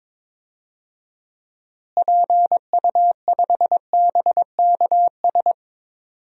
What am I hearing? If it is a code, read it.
Morse code, heard as a keyed tone, about 22 wpm, 700 Hz. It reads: PU5BKS
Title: DXCC-100 Call Signs - 6 Characters - Repeated 3x - 22wpm